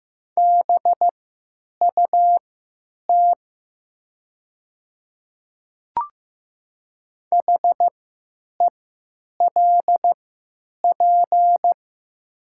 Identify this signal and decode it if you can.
Morse code, heard as a keyed tone, about 15 wpm, spaced as for 9 wpm, 700 Hz. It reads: BUT HELP
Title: 3x Character Spacing - Top 200 Words - 15wpm